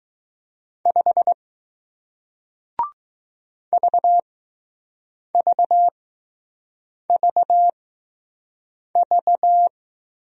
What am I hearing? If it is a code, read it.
Morse code, heard as a keyed tone, about 20 wpm, spaced as for 6 wpm, 700 Hz. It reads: 5 VVVV